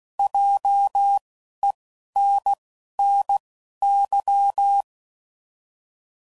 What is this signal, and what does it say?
Morse code, heard as a keyed tone, about 16 wpm, spaced as for 11 wpm, 800 Hz. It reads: JENNY